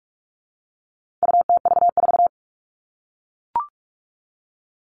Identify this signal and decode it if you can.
Morse code, heard as a keyed tone, about 44 wpm, 700 Hz. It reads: UTV4
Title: All Cut Numbers - 4-Digit 45wpm